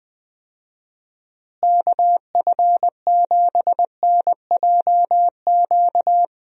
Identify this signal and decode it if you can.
Morse code, heard as a keyed tone, about 21 wpm, 700 Hz. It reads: KF7NJQ
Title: USA Call Signs - 6 Characters - Repeated 3x - 20wpm